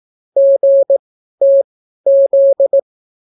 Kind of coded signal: Morse code